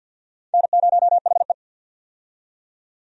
Morse code, heard as a keyed tone, about 52 wpm, 700 Hz. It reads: N0HE
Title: USA Call Signs - 4 Characters - 50wpm